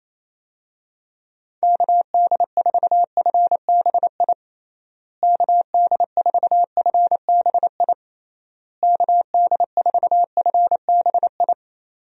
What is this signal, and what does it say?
Morse code, heard as a keyed tone, about 27 wpm, 700 Hz. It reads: KD4FBI KD4FBI KD4FBI
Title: USA Call Signs - 6 Characters - Repeated 3x - 28wpm